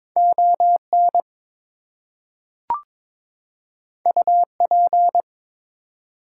Morse code, heard as a keyed tone, about 23 wpm, 700 Hz. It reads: ON UP
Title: Two Letter Words - 22wpm